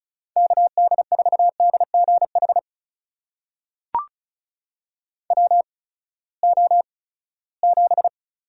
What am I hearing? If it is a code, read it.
Morse code, heard as a keyed tone, about 36 wpm, spaced as for 20 wpm, 700 Hz. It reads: KD4DGH WO7
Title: USA Call Signs - 8x Character Spacing - 35wpm